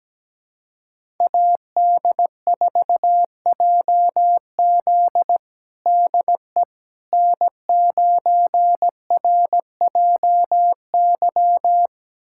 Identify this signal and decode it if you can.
Morse code, heard as a keyed tone, about 17 wpm, 700 Hz. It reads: AD4JZ DE N9RJY
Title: USA Callsign DE Callsign - 5 Characters - 17wpm